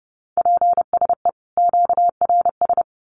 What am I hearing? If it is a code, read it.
Morse code, heard as a keyed tone, about 30 wpm, 700 Hz. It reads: PSE QRS
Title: Common QSO Phrases 30wpm